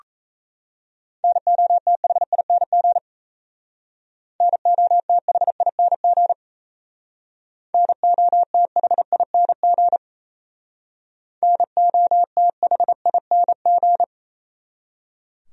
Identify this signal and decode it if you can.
Morse code, heard as a keyed tone, about 35 wpm, 700 Hz. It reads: NOTHING NOTHING NOTHING NOTHING